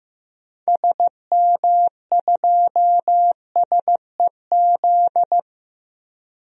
Morse code, heard as a keyed tone, about 15 wpm, 700 Hz. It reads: SM2SEZ
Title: DXCC-100 Call Signs - 6 Characters - 15wpm